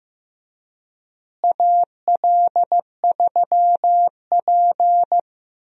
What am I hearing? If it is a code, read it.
Morse code, heard as a keyed tone, about 15 wpm, 700 Hz. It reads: AL3P
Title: USA Call Signs - 4 Characters - Repeated 3x - 15wpm